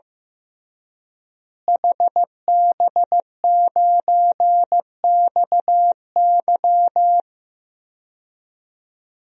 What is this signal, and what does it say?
Morse code, heard as a keyed tone, about 15 wpm, 700 Hz. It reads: HB9XY
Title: DXCC-100 Call Signs - 5 Characters - Repeated 3x - 15wpm